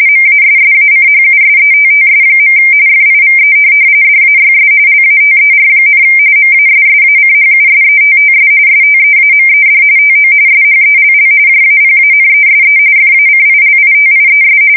RTTY, about 45.45 baud. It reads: MASSACHUSETTS. FIVE DAYS LATER, MASSACHUSETTS STEAMED TO SHELL JAPANESE FACILITIES
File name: fsk170 / USS-Massachusetts